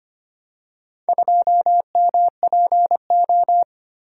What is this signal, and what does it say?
Morse code, heard as a keyed tone, about 25 wpm, 700 Hz. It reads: 2MPO
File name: practice / 25wpm / Four Letter-Number Combinations - 25wpm